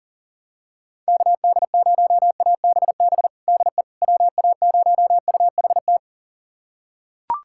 Morse code, heard as a keyed tone, about 37 wpm, 700 Hz. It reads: KD0ABB DE WA0UHT
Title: USA Callsign DE Callsign - 6 Characters - 40wpm